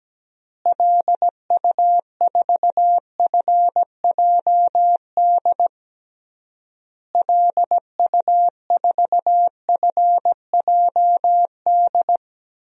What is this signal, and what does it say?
Morse code, heard as a keyed tone, about 17 wpm, 700 Hz. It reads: LU4FJD LU4FJD
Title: DXCC-100 Call Signs - 6 Characters - Repeated 3x - 17wpm